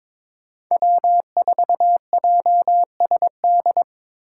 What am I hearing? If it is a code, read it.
Morse code, heard as a keyed tone, about 25 wpm, 700 Hz. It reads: W4JSD
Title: Speed Racing - USA 1x3 Call Signs 22wpm